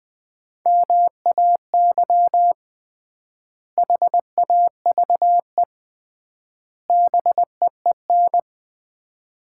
Morse code, heard as a keyed tone, about 20 wpm, 700 Hz. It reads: MAY HAVE BEEN
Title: 3x Word-Spacing - Sets of 3 Words - 20wpm